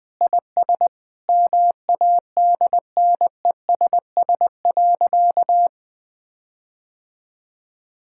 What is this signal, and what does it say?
Morse code, heard as a keyed tone, about 21 wpm, 700 Hz. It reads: IS MADNESS.
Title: Sentences with 7-Word Phrases - 20wpm